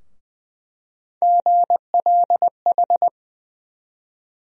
Morse code, heard as a keyed tone, about 21 wpm, 700 Hz. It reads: GLH
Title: USA Call Sign Postfixes - 3 Letters 20wpm